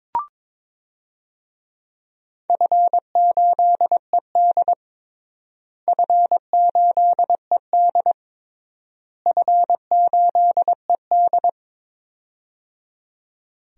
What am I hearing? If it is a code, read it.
Morse code, heard as a keyed tone, about 22 wpm, 700 Hz. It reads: F8ED F8ED F8ED
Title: DXCC-100 Call Signs - 4 Characters - Repeated 3x - 22wpm